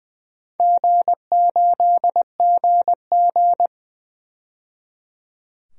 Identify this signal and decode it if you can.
Morse code, heard as a keyed tone, about 21 wpm, 700 Hz. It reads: G8GG
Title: DXCC-100 Call Signs - 4 Characters - 20wpm